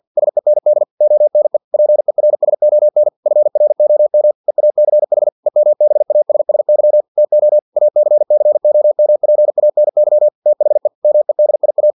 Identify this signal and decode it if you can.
Morse code, heard as a keyed tone, about 50 wpm, 600 Hz. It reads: SEND ONE PERSON FROM EACH EMBASSY TO ACCOMPANY THE MEDIA
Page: Never 49wpm